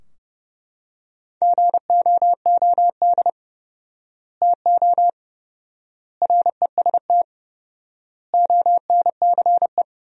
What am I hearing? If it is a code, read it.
Morse code, heard as a keyed tone, about 30 wpm, 700 Hz. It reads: GOOD TO REST ONCE
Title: 4x Word-Spacing - Sentences from Top 500 Words - 30wpm